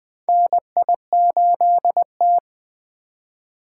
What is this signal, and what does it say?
Morse code, heard as a keyed tone, about 20 wpm, 700 Hz. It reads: NI8T
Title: USA Call Signs - 4 Characters - 3x Character Spacing - 20wpm